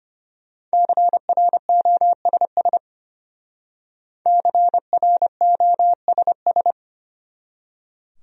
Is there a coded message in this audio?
Yes: Morse code from 0.7 to 6.7 s, heard as a keyed tone, about 31 wpm, 700 Hz. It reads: CROSS CROSS